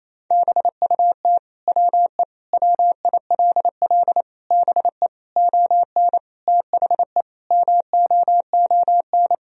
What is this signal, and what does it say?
Morse code, heard as a keyed tone, about 28 wpm, 700 Hz. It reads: BUT WE WILL BE ON THE MOON
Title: Sentences from Top 1000 Words - Review Part 3 - 28wpm